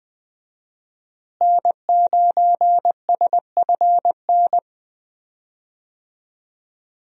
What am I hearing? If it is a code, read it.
Morse code, heard as a keyed tone, about 20 wpm, 700 Hz. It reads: N9SFN
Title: USA Call Signs - 5 Characters - Repeated 3x - 20wpm